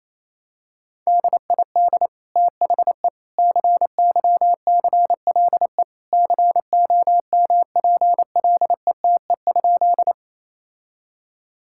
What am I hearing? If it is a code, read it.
Morse code, heard as a keyed tone, about 28 wpm, 700 Hz. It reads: DID THE CYCLE COMPLETE?
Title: Sentences from Top 1000 Words - Review Part 6 - 28wpm